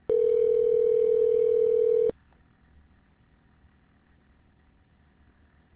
Ringback tone (North America), 440 + 480 Hz.